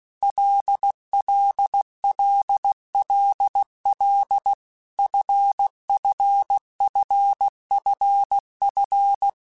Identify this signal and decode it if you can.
Morse code, heard as a keyed tone, about 16 wpm, 800 Hz. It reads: LLLLL FFFFF
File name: Basic Letter Groups By7Groups / LFCYQ / LFCYQ@15